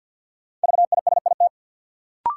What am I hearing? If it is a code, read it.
Morse code, heard as a keyed tone, about 53 wpm, 700 Hz. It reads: VISIT
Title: Sentences from Top 900 Words - New Word Drill - 50wpm